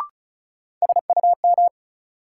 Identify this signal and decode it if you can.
Morse code, heard as a keyed tone, about 36 wpm, 700 Hz. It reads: SUM
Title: Sentences from Top 1000 Words - Review Part 6 - New Word Drill - 35wpm